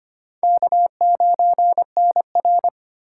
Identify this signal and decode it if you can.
Morse code, heard as a keyed tone, about 25 wpm, 700 Hz. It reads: K9NR
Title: K1USN SST QSO Practice - 25wpm